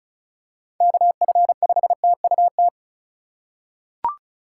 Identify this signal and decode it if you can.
Morse code, heard as a keyed tone, about 36 wpm, 700 Hz. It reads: KF5TUT
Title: USA Call Signs - 6 Characters - 3x Character Spacing - 35wpm